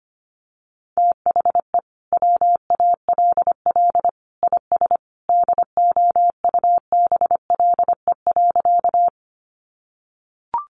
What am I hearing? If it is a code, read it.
Morse code, heard as a keyed tone, about 25 wpm, 700 Hz. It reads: THE WALL IS DOUBLE.
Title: Sentences from Top 1400 Words 25wpm